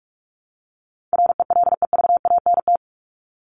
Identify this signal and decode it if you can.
Morse code, heard as a keyed tone, about 44 wpm, 700 Hz. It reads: RELEVANT
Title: Top 1001-1250 Words 45wpm